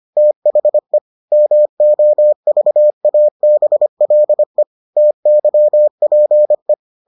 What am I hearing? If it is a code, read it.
Morse code, heard as a keyed tone, about 25 wpm, 600 Hz. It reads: THE MOVABLE TYPE